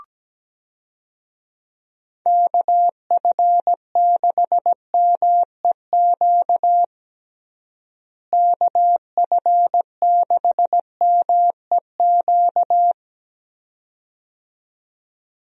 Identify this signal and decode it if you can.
Morse code, heard as a keyed tone, about 17 wpm, 700 Hz. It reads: KF6MEQ KF6MEQ
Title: USA Call Signs - 6 Characters - Repeated 2x - 17wpm